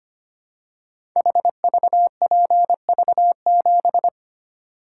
Morse code, heard as a keyed tone, about 25 wpm, 700 Hz. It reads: HVPV7
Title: Code Groups - 5 Letter:Numbers - 25wpm